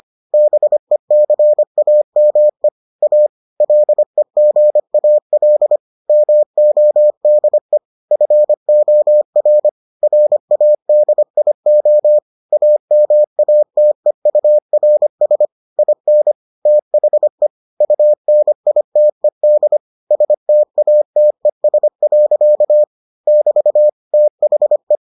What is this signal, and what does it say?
Morse code, heard as a keyed tone, about 25 wpm, 600 Hz. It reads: BECAME A LEGAL MODE FOR RADIO AMATEURS IN THE UNITED STATES. <BT> THE